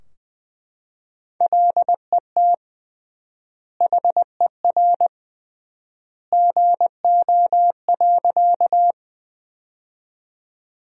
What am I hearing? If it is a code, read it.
Morse code, heard as a keyed tone, about 21 wpm, 700 Hz. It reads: LET HER GO.
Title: 3x Word-Spacing - Sentences from Top 300 Words - 20wpm